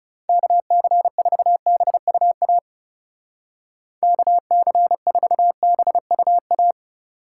Morse code, heard as a keyed tone, about 38 wpm, 700 Hz. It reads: KC4BUA KC4BUA